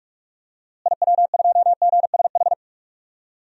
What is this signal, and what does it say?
Morse code, heard as a keyed tone, about 46 wpm, 700 Hz. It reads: IW2GSH
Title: DXCC-100 Call Signs - 6 Characters - Repeated 3x - 45wpm